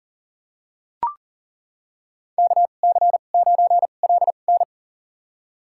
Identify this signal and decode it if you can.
Morse code, heard as a keyed tone, about 40 wpm, spaced as for 29 wpm, 700 Hz. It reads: KC9LN
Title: USA Call Signs - 5 Characters - 2x Character Spacing - 40wpm